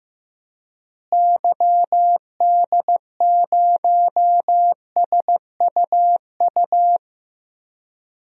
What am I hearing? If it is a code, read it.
Morse code, heard as a keyed tone, about 15 wpm, 700 Hz. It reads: YD0SUU
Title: DXCC-100 Call Signs - 6 Characters - Repeated 2x - 15wpm